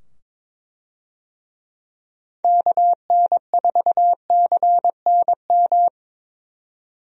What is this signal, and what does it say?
Morse code, heard as a keyed tone, about 22 wpm, 700 Hz. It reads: KN4CNM
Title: USA Call Signs - 6 Characters - 22wpm